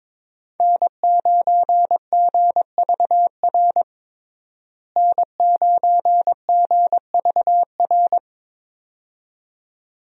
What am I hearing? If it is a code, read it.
Morse code, heard as a keyed tone, about 21 wpm, 700 Hz. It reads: N9GVR N9GVR
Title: USA Call Signs - 5 Characters - Repeated 2x - 22wpm